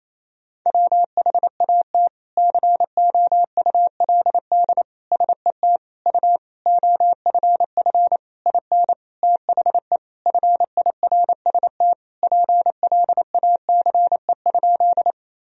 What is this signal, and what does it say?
Morse code, heard as a keyed tone, about 28 wpm, 700 Hz. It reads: WHAT COULD SET U OFF IN THE FIRST PLACE?
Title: Sentences from Top 200 Words and Abbreviations 28wpm